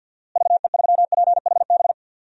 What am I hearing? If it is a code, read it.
Morse code, heard as a keyed tone, about 51 wpm, 700 Hz. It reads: VE3PHB
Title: DXCC-100 Call Signs - 6 Characters - Repeated 3x - 50wpm